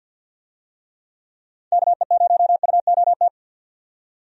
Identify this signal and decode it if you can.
Morse code, heard as a keyed tone, about 50 wpm, 700 Hz. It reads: KE0UOT
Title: USA Call Signs - 6 Characters - 50wpm